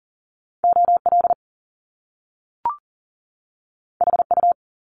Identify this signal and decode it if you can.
Morse code, heard as a keyed tone, about 42 wpm, 700 Hz. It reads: OL HU
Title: Top Two Letter Combinations in Words 40wpm